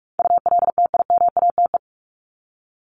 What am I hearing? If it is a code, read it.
Morse code, heard as a keyed tone, about 44 wpm, 700 Hz. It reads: ULTIMATE
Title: LY Suffix - Top 100 Words with Base Words 45wpm